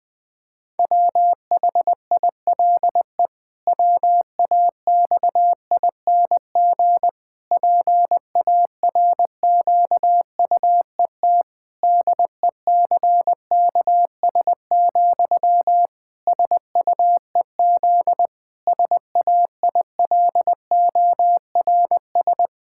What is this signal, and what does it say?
Morse code, heard as a keyed tone, about 20 wpm, 700 Hz. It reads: WHILE WAXING PARQUET DECKS, SUEZ SAILORS
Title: Pangram Challenge v2 - 20wpm